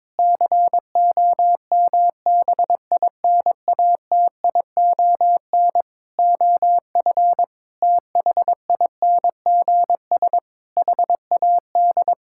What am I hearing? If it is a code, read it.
Morse code, heard as a keyed tone, about 22 wpm, 700 Hz. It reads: COMBINATION OF THINGS HAD
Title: Sentences from Top 1000 Words - Review Part 6 - 22wpm